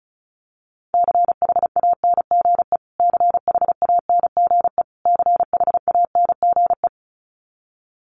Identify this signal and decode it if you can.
Morse code, heard as a keyed tone, about 34 wpm, 700 Hz. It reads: CHANGE CHANGE CHANGE
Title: Top 500 Words Repeated 35wpm